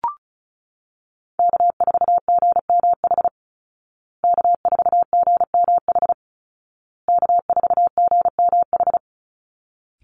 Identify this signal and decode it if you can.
Morse code, heard as a keyed tone, about 34 wpm, 700 Hz. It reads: K4GMH K4GMH K4GMH
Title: Call Signs Repeated 2x - 35wpm